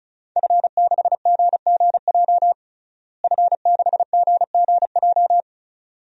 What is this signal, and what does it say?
Morse code, heard as a keyed tone, about 34 wpm, 700 Hz. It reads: F6GGJ F6GGJ
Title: DXCC-100 Call Signs - 5 Characters - Repeated 2x - 35wpm